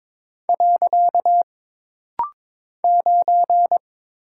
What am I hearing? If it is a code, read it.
Morse code, heard as a keyed tone, about 22 wpm, 700 Hz. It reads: . 9
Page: This rapid-fire timing is geared to push your ICR (Instant Character Recognition) skills.